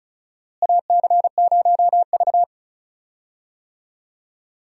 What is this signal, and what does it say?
Morse code, heard as a keyed tone, about 36 wpm, 700 Hz. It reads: AC0V
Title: USA Call Signs - 4 Characters - Repeated 3x - 35wpm